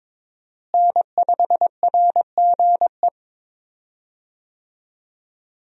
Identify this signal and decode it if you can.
Morse code, heard as a keyed tone, about 22 wpm, 700 Hz. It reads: N5RGE
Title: USA Call Signs - 5 Characters - Repeated 2x - 22wpm